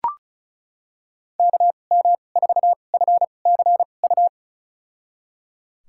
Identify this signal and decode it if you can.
Morse code, heard as a keyed tone, about 36 wpm, spaced as for 25 wpm, 700 Hz. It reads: KM4FCU
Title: USA Call Signs - 6 Characters - 2x Character Spacing - 35wpm